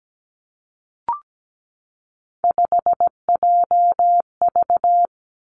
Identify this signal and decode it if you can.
Morse code, heard as a keyed tone, about 17 wpm, 700 Hz. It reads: 5JV
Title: Three Challenging Letter-Number Combinations 17wpm